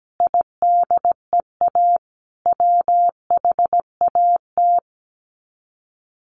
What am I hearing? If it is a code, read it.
Morse code, heard as a keyed tone, about 17 wpm, 700 Hz. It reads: IDEA WHAT
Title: Sets of 5 Words Encore 17wpm